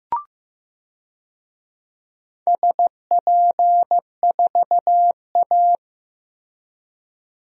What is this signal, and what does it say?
Morse code, heard as a keyed tone, about 15 wpm, 700 Hz. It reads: SP4A
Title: DXCC-100 Call Signs - 4 Characters - Repeated 3x - 15wpm